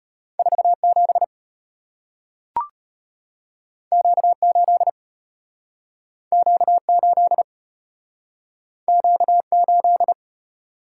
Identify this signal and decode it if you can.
Morse code, heard as a keyed tone, about 35 wpm, 700 Hz. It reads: 47 Q8 Q8 Q8